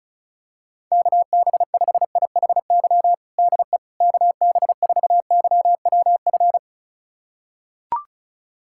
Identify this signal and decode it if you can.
Morse code, heard as a keyed tone, about 36 wpm, 700 Hz. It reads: KB5IHY DE KB4YWF
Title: USA Callsign DE Callsign - 6 Characters - 35wpm